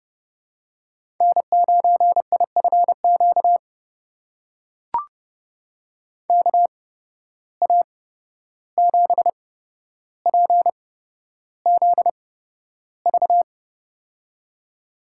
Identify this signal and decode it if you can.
Morse code, heard as a keyed tone, about 32 wpm, spaced as for 12 wpm, 700 Hz. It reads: N9IFQ KA7PZV
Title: USA Call Signs - 8x Character Spacing - 30wpm